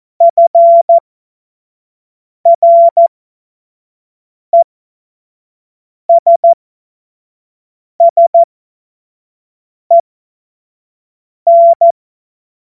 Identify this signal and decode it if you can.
Morse code, heard as a keyed tone, about 14 wpm, spaced as for 5 wpm, 680 Hz. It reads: FRESSEN
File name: Witz-23-September-2025-25-F70